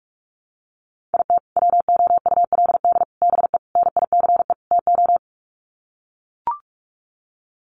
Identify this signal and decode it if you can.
Morse code, heard as a keyed tone, about 44 wpm, 700 Hz. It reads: IT WOULD BE NICE TO
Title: Sets of 5 Words Encore2 45wpm